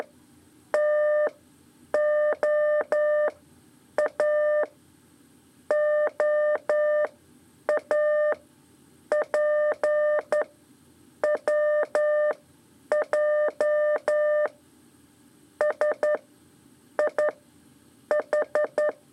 Morse code, heard as a keyed tone, about 11 wpm, 600 Hz. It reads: T O A O A P W J S I H
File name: 3 lezione CW 2 AIR